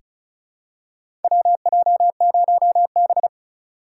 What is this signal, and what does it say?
Morse code, heard as a keyed tone, about 36 wpm, 700 Hz. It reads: WJ0B
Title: USA Call Signs - 4 Characters - 2x Character Spacing - 35wpm